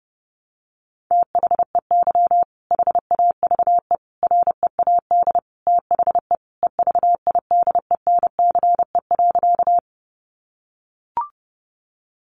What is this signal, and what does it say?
Morse code, heard as a keyed tone, about 30 wpm, 700 Hz. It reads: THEY HAVE READ THE EVIDENCE.
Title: Sentences from Top 1300 Words 30wpm